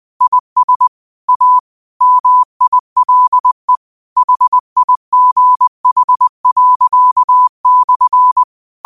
Morse code, heard as a keyed tone, about 20 wpm, 1000 Hz. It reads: IS A MILE HIGH./
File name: qotdfiles / 20wpm